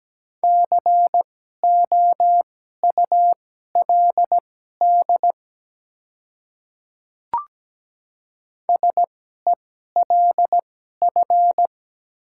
Morse code, heard as a keyed tone, about 17 wpm, spaced as for 12 wpm, 700 Hz. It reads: COULD SELF
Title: 2x Character Spacing - Top 300 Words - 17wpm